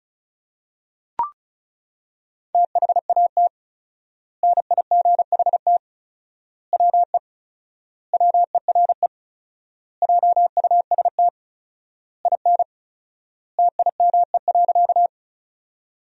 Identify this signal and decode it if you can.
Morse code, heard as a keyed tone, about 36 wpm, 700 Hz. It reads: THAT NIGHT WE WERE JUST IN TIME.
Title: 4x Word-Spacing - Sentences from Top 300 Words - 35wpm